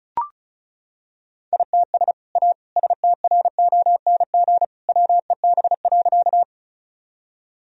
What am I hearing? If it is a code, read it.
Morse code, heard as a keyed tone, about 34 wpm, 700 Hz. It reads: ITS A STRONG WEB.
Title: Sentences from Top 1000 Words - Review Part 4 - 35wpm